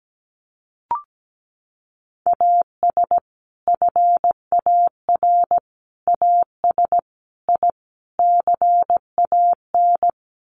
Morse code, heard as a keyed tone, about 17 wpm, 700 Hz. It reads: AS FAR AS I CAN
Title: Sets of 5 Words Encore 17wpm